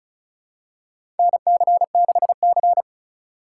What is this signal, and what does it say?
Morse code, heard as a keyed tone, about 34 wpm, 700 Hz. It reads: NC6C
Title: USA Call Signs - 4 Characters - 3x Character Spacing - 35wpm